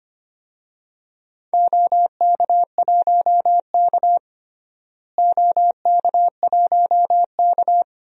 Morse code, heard as a keyed tone, about 25 wpm, 700 Hz. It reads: OK1K OK1K
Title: DXCC-100 Call Signs - 4 Characters - Repeated 3x - 25wpm